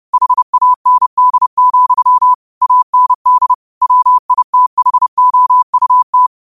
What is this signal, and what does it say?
Morse code, heard as a keyed tone, about 30 wpm, 1000 Hz. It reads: HAND, AND WITHOUT
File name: qotdfiles / 30wpm